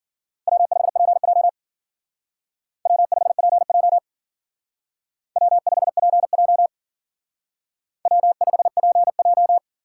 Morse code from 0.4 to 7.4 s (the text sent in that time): W5PJ W5PJ W5PJ